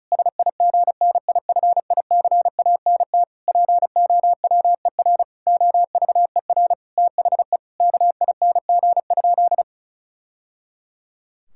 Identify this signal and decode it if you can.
Morse code, heard as a keyed tone, about 34 wpm, 700 Hz. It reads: SIGNIFICANT POWER OVER THE KING?
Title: Sentences from Top 1000 Words - Review Part 6 - 35wpm